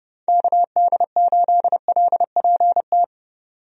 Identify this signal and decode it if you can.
Morse code, heard as a keyed tone, about 30 wpm, 700 Hz. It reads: KD8LPT
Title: USA Call Signs - 6 Characters - 30wpm